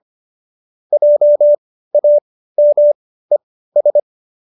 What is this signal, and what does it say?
Morse code, heard as a keyed tone, about 25 wpm, spaced as for 15 wpm, 590 Hz. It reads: JAMES